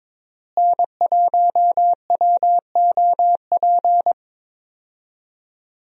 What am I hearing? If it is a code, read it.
Morse code, heard as a keyed tone, about 21 wpm, 700 Hz. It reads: N1WOP
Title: USA Call Signs - 5 Characters - Repeated 3x - 22wpm